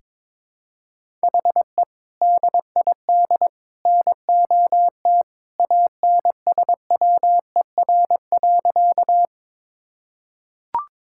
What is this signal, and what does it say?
Morse code, heard as a keyed tone, about 22 wpm, 700 Hz. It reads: HE DID NOT ANSWER.
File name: practice / 22wpm / Sentences with 3-Word Phrases - 22wpm